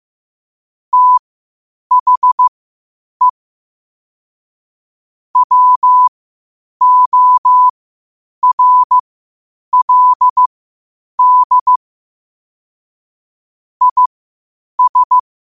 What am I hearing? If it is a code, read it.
Morse code, heard as a keyed tone, about 15 wpm, spaced as for 8 wpm, 1000 Hz. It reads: THE WORLD IS
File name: qotdfiles / 8wpm